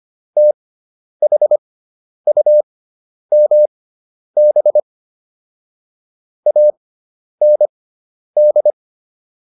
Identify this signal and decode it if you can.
Morse code, heard as a keyed tone, about 25 wpm, spaced as for 10 wpm, 600 Hz. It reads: THUMB AND